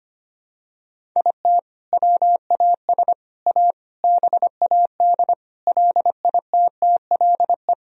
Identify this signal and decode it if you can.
Morse code, heard as a keyed tone, about 25 wpm, 700 Hz. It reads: IT WAS A BAD LITTLE
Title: Sentences from Top 1000 Words - Review Part 1 - 25wpm